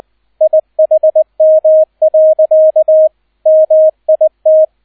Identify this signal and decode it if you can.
Morse code, heard as a keyed tone, about 20 wpm, 630 Hz, with radio noise. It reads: IHM. MIT
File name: 002 Zweiter Streich 20wpm